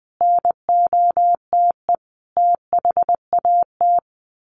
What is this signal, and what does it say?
Morse code, heard as a keyed tone, about 20 wpm, 700 Hz. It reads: NOTE THAT
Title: Sets of 5 Words Encore2 20wpm